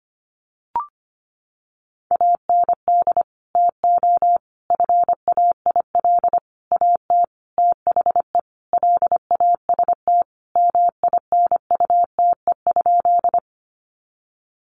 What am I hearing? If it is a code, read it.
Morse code, heard as a keyed tone, about 25 wpm, 700 Hz. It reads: AND TO FAIL AT THE LAST MINUTE?
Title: Sentences from Top 1100 Words 25wpm